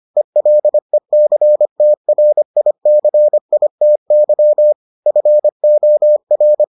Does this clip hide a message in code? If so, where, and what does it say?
Yes: Morse code from 0.2 to 6.6 s, heard as a keyed tone, about 25 wpm, 600 Hz. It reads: ELECTRICITY FOR